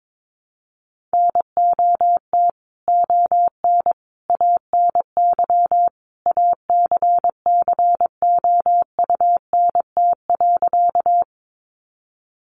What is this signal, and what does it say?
Morse code, heard as a keyed tone, about 22 wpm, 700 Hz. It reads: NOT ON ANY ACCOUNT.
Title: Sentences from Top 900 Words 22wpm